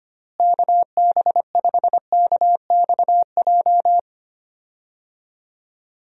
Morse code, heard as a keyed tone, about 25 wpm, 700 Hz. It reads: KB5KXJ
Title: USA Call Signs - 6 Characters - Repeated 3x - 25wpm